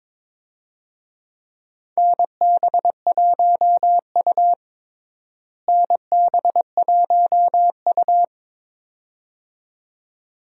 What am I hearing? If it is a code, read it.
Morse code, heard as a keyed tone, about 22 wpm, 700 Hz. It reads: NB1U NB1U
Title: USA Call Signs - 4 Characters - Repeated 2x - 22wpm